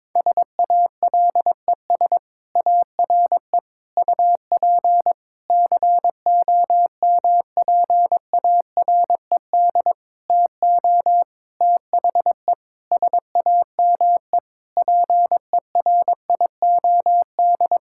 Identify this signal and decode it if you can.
Morse code, heard as a keyed tone, about 22 wpm, 700 Hz. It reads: SALES ARE UP COMPARED TO THE SAME PERIOD
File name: Sentences with 7-Word Phrases - 22wpm